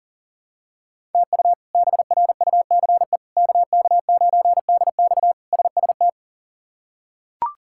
Morse code, heard as a keyed tone, about 39 wpm, 700 Hz. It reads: TU BRUCE KK9DX SST
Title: K1USN SST QSO Practice - 40wpm